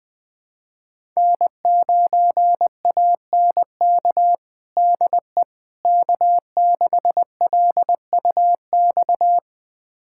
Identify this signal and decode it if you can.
Morse code, heard as a keyed tone, about 20 wpm, 700 Hz. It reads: N9ANK DE K6LUX
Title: USA Callsign DE Callsign 20wpm